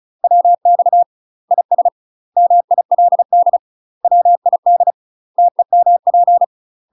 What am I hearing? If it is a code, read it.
Morse code, heard as a keyed tone, about 36 wpm, 700 Hz. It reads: WX IS MILD WID TEMP
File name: UK MASTER QSOs TRAINING / 35WPM / UK MASTER QSOs 0041